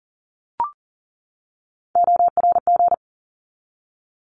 Morse code, heard as a keyed tone, about 40 wpm, 700 Hz. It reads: ORG
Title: ARRL Sections 40wpm